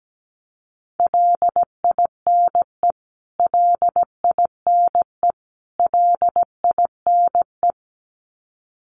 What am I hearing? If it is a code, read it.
Morse code, heard as a keyed tone, about 17 wpm, 700 Hz. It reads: LINE LINE LINE
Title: Top 200 Words Repeated 17wpm